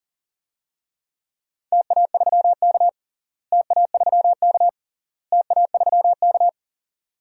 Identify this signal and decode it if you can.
Morse code, heard as a keyed tone, about 40 wpm, 700 Hz. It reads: TA3K TA3K TA3K
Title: DXCC-100 Call Signs - 4 Characters - Repeated 3x - 40wpm